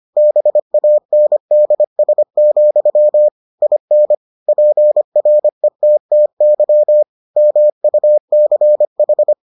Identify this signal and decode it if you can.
Morse code, heard as a keyed tone, about 25 wpm, 600 Hz. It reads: BANDS, IN PRETTY MUCH